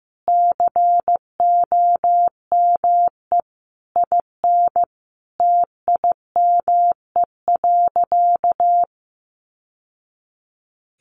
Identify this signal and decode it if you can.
Morse code, heard as a keyed tone, about 15 wpm, 700 Hz. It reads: COME IN TIME.
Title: Sentences from Top 600 Words 15wpm